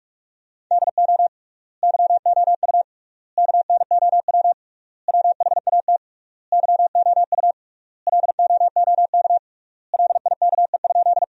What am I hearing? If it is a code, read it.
Morse code, heard as a keyed tone, about 46 wpm, 700 Hz. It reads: DO YOU KNOW WHAT YOU LOOK LIKE?
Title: 3x Word-Spacing - Sentences from Top 100 Words - 45wpm